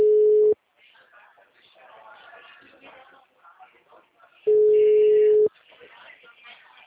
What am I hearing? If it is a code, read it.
Ringback tone (Europe, 425 Hz), with speech.
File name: out- -2002- - - .16079